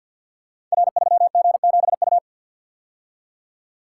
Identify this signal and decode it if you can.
Morse code, heard as a keyed tone, about 50 wpm, 700 Hz. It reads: R3G7U
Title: Code Groups - 5 Letter:Numbers - 50wpm